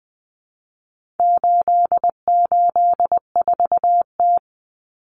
Morse code, heard as a keyed tone, about 20 wpm, 700 Hz. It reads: 884T
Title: Common Cut Numbers - 4-Digit 20wpm